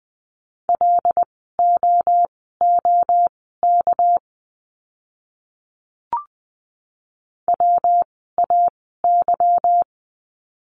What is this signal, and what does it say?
Morse code, heard as a keyed tone, about 20 wpm, spaced as for 15 wpm, 700 Hz. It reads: LOOK WAY
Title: 237 Lesson 38f - Letter q - Words 20wpm